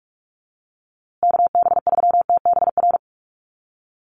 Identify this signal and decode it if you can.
Morse code, heard as a keyed tone, about 46 wpm, 690 Hz. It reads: KB3TBR
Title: USA Call Signs Repeated 3x 45wpm